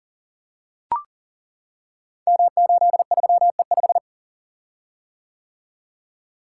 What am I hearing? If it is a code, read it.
Morse code, heard as a keyed tone, about 38 wpm, 700 Hz. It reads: M83E5
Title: Code Groups - 5 Letter:Numbers - 40wpm